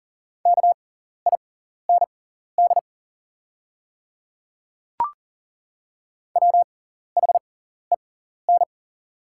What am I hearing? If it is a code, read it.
Morse code, heard as a keyed tone, about 38 wpm, spaced as for 14 wpm, 700 Hz. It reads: KIND WHEN
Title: 6x Character Spacing - Top 200 Words - 40wpm